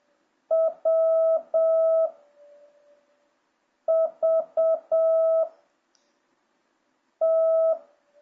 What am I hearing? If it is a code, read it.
Morse code, heard as a keyed tone, about 7 wpm, 660 Hz. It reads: W V T